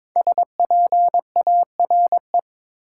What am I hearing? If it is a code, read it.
Morse code, heard as a keyed tone, about 22 wpm, 700 Hz. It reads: SPARE
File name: 5 Letter Words - First 3 Letters Given - 22wpm